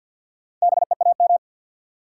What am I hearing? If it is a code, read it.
Morse code, heard as a keyed tone, about 54 wpm, 700 Hz. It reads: BEAM
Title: Top 180 QSO Elements 50wpm